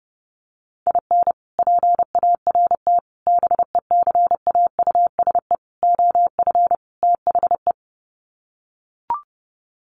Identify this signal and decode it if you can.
Morse code, heard as a keyed tone, about 30 wpm, 700 Hz. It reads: IN PART BECAUSE OF THE
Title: Sets of 5 Words Encore2 30wpm